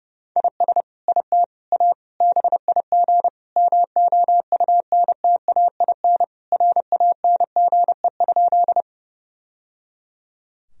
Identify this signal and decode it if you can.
Morse code, heard as a keyed tone, about 32 wpm, 700 Hz. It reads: IS IT A BIG MOUNTAIN RANGE?
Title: Sentences from Top 1000 Words - Review Part 4 - 30wpm